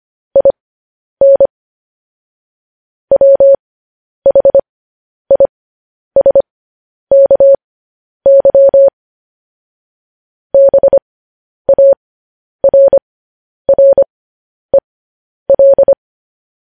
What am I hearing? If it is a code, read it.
Morse code, heard as a keyed tone, about 25 wpm, spaced as for 10 wpm, 550 Hz. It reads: IN WHISKY BARREL
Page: news-headlines-in-morse-code-at-10-wpm